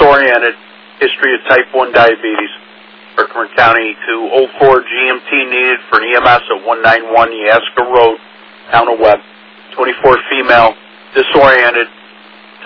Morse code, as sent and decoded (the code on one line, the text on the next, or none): none